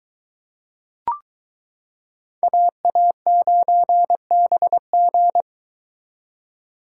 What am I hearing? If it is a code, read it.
Morse code, heard as a keyed tone, about 22 wpm, 700 Hz. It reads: AA9BG